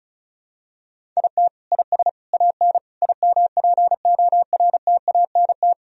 Morse code, heard as a keyed tone, about 36 wpm, 700 Hz. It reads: IT IS AN IMPORTANT
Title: Sentences from Top 1000 Words - Review Part 1 - 35wpm